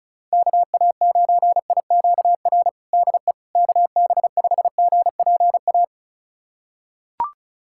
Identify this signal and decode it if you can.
Morse code, heard as a keyed tone, about 34 wpm, 700 Hz. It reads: KA9IQR DE KB5GPA
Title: USA Callsign DE Callsign - 6 Characters - 35wpm